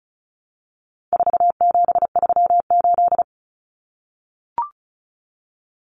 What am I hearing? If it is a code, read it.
Morse code, heard as a keyed tone, about 34 wpm, 700 Hz. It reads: 4738
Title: Numbers - 4-Digit 35wpm